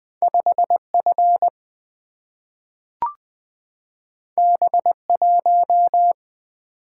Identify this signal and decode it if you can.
Morse code, heard as a keyed tone, about 20 wpm, 700 Hz. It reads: 5 F B 1
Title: Single-Letter Single-Number 20wpm